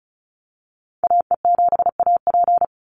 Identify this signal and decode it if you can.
Morse code, heard as a keyed tone, about 34 wpm, 700 Hz. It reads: AE7AP
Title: NA SOTA Chaser Callsigns 35wpm